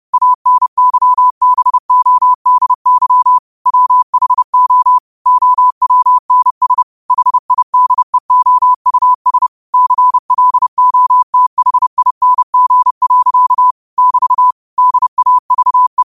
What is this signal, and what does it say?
Morse code, heard as a keyed tone, about 30 wpm, 1000 Hz. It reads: ANYBODY WHO OWNS HIDEOUS CLOTHING. <BT> DAVE
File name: qotdfiles / 30wpm